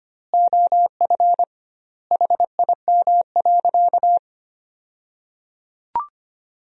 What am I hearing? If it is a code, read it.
Morse code, heard as a keyed tone, about 25 wpm, 700 Hz. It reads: OF HIM.
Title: 2x Word-Spacing - Sentences from Top 500 Words - 25wpm